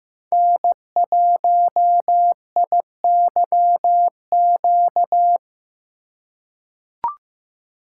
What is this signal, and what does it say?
Morse code, heard as a keyed tone, about 15 wpm, 700 Hz. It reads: N1IYQ
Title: USA Call Signs - 5 Characters - 3x Character Spacing - 15wpm